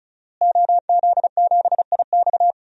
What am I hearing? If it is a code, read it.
Morse code, heard as a keyed tone, about 36 wpm, 700 Hz. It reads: OZ7IX
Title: DXCC-100 Call Signs - 5 Characters - Repeated 3x - 35wpm